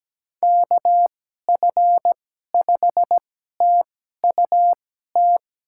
Morse code, heard as a keyed tone, about 17 wpm, spaced as for 12 wpm, 700 Hz. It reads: KF5TUT
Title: USA Call Signs - 6 Characters - 2x Character Spacing - 17wpm